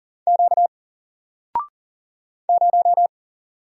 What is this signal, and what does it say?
Morse code, heard as a keyed tone, about 40 wpm, 700 Hz. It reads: Q 0
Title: Single Character (Letter:Number:Punctuation:Prosign) 40wpm